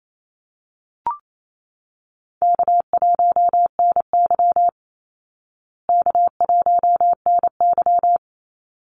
Morse code, heard as a keyed tone, about 28 wpm, 700 Hz. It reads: K1NY K1NY
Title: Easy Call Signs Repeated 2x - 28wpm